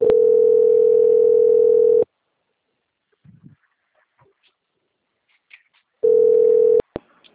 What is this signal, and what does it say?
Ringback tone (North America), 440 + 480 Hz.
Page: Pavuna - RJ